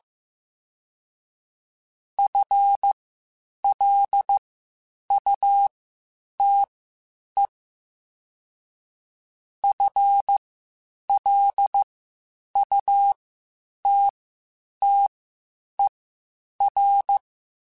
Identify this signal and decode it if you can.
Morse code, heard as a keyed tone, about 15 wpm, spaced as for 8 wpm, 800 Hz. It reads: FLUTE FLUTTER